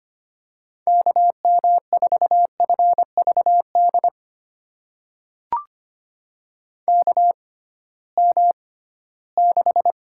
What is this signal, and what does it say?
Morse code, heard as a keyed tone, about 25 wpm, spaced as for 16 wpm, 700 Hz. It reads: KM4FVD KM6
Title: USA Call Signs - 6 Characters - 6x Character Spacing - 25wpm